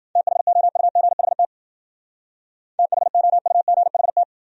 Morse code, heard as a keyed tone, about 52 wpm, 700 Hz. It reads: THOUGHT THOUGHT